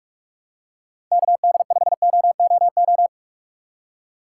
Morse code, heard as a keyed tone, about 46 wpm, 700 Hz. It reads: KD5OOO
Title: USA Call Signs - 6 Characters - 2x Character Spacing - 45wpm